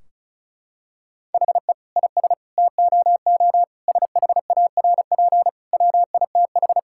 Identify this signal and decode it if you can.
Morse code, heard as a keyed tone, about 36 wpm, 700 Hz. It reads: HE IS TOO SHARP WITH
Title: Sentences from Top 1600 Words 35wpm